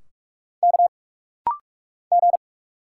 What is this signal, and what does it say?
Morse code, heard as a keyed tone, about 46 wpm, 700 Hz. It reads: K G
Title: Single Letters - Rapid-Fire 45wpm